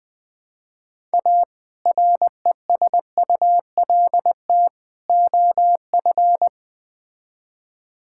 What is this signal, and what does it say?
Morse code, heard as a keyed tone, about 20 wpm, 700 Hz. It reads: A RESULT OF
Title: Build A Phrase - 3-Word Phrases - 20wpm